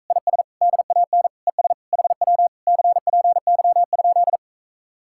Morse code, heard as a keyed tone, about 45 wpm, 700 Hz. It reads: IS DAN ES HW CPY?